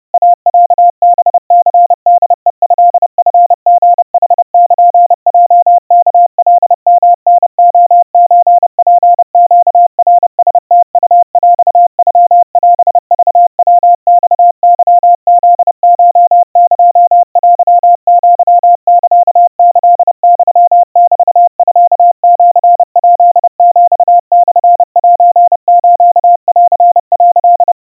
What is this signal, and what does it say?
Morse code (Japanese Wabun code), heard as a keyed tone, about 30 wpm, 700 Hz. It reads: イロハニホヘトチリヌルヲワカヨタレソツネナラムウヰノオクヤマケフコエテアサキユメミシヱヒモセスン」
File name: iroha 100 cpm 700 Hz Crispy